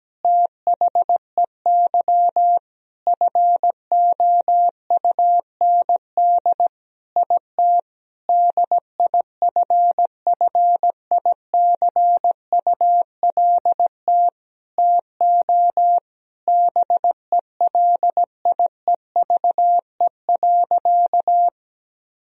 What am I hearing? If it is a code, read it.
Morse code, heard as a keyed tone, about 17 wpm, 700 Hz. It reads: THEY FOUND IT DIFFICULT TO BELIEVE.
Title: Sentences from Top 1000 Words - Review Part 4 - 17wpm